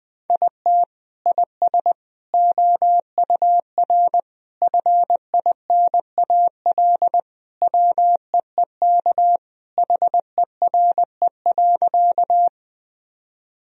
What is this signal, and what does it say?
Morse code, heard as a keyed tone, about 21 wpm, 700 Hz. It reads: IT IS OUR FINAL WEEK HERE.
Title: Sentences from Top 1000 Words - Review Part 2 - 20wpm